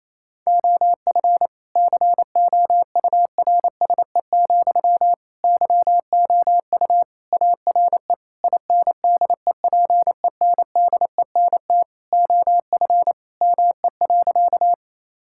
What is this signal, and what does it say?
Morse code, heard as a keyed tone, about 28 wpm, 700 Hz. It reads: OF COURSE, YOU ARE INDEPENDENT OF ME.
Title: Sentences from Top 1000 Words - Review Part 6 - 28wpm